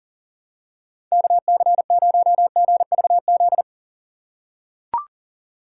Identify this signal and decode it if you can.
Morse code, heard as a keyed tone, about 40 wpm, 700 Hz. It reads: KC0GVZ
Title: USA Call Signs - 3x Character Spacing - 40wpm